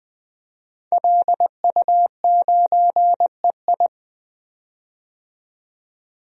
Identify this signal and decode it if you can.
Morse code, heard as a keyed tone, about 21 wpm, 700 Hz. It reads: LU9EI
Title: DXCC-100 Call Signs - 5 Characters - 20wpm